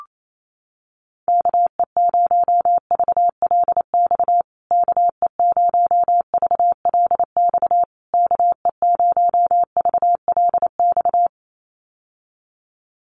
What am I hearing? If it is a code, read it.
Morse code, heard as a keyed tone, about 28 wpm, 700 Hz. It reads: KE0VLX KE0VLX KE0VLX
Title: USA Call Signs Repeated 3x 28wpm